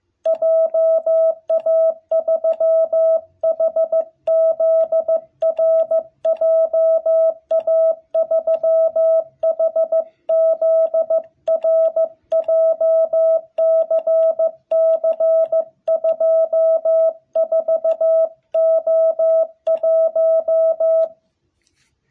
Morse code, heard as a keyed tone, about 15 wpm, 660 Hz. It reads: JA3HZR JA3HZR JCC24O1
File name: jh8cbh / 2021cw